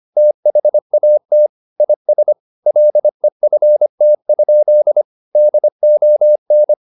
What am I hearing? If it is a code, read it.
Morse code, heard as a keyed tone, about 25 wpm, 600 Hz. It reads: THAT IS LEFT? DON
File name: the old man and the sea part 1 25wpm